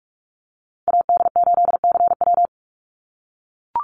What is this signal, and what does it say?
Morse code, heard as a keyed tone, about 44 wpm, 700 Hz. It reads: AD8CW
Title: Call Signs - 45wpm